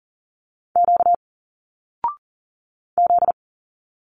Morse code, heard as a keyed tone, about 40 wpm, 700 Hz. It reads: Q Z
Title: Single Letters 40wpm